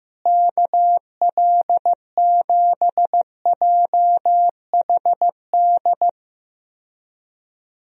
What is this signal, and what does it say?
Morse code, heard as a keyed tone, about 14 wpm, 700 Hz. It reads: KL7JHD
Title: USA Call Signs - 6 Characters - Repeated 3x - 15wpm